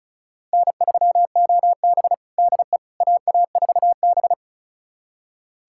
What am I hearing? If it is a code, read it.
Morse code, heard as a keyed tone, about 36 wpm, 700 Hz. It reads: N3OB DE AA4B
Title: USA Callsign DE Callsign - 4 Characters - 35wpm